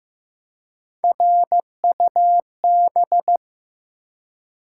Morse code, heard as a keyed tone, about 15 wpm, 700 Hz. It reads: RUB
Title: Three Letter Words - 15wpm